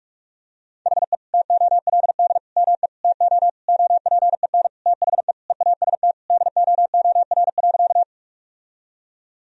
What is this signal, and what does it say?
Morse code, heard as a keyed tone, about 47 wpm, 700 Hz. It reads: HE TOLD ME TO OPEN THE EAST DOOR.
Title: Sentences from Top 1000 Words - Review Part 3 - 45wpm